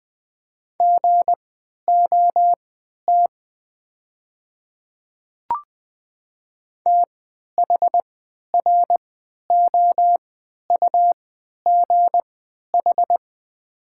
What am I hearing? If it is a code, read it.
Morse code, heard as a keyed tone, about 20 wpm, spaced as for 11 wpm, 690 Hz. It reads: GOT THROUGH
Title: 3x Character Spacing - Top 300 Words - 20wpm